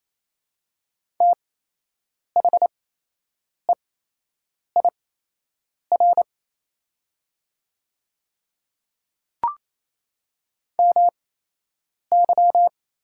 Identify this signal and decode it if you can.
Morse code, heard as a keyed tone, about 28 wpm, spaced as for 8 wpm, 700 Hz. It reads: THEIR MY
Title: 8x Character Spacing - Top 100 Words - 28wpm